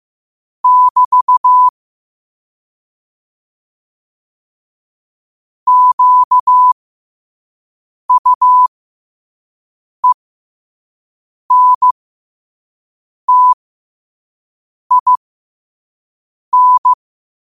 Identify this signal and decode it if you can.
Morse code, heard as a keyed tone, about 15 wpm, spaced as for 5 wpm, 1000 Hz. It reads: <BT> QUENTIN
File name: qotdfiles / 5wpm